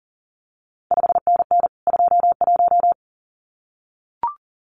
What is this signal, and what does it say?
Morse code, heard as a keyed tone, about 38 wpm, 700 Hz. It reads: 5NN 21
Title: CQ WW DX Contest Exchange Encore - 40wpm